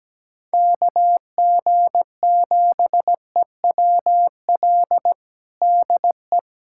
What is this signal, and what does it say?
Morse code, heard as a keyed tone, about 17 wpm, 700 Hz. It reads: KG7EWL DE
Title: USA Callsign DE Callsign - 6 Characters - 17wpm